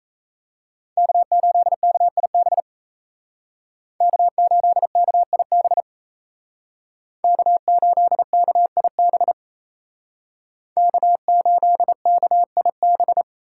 Morse code, heard as a keyed tone, about 36 wpm, 700 Hz. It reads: K8KIB K8KIB K8KIB K8KIB